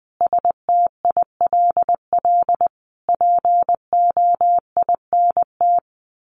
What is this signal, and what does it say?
Morse code, heard as a keyed tone, about 20 wpm, 700 Hz. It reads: STILL POINT
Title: Sentences from Top 1500 Words 20wpm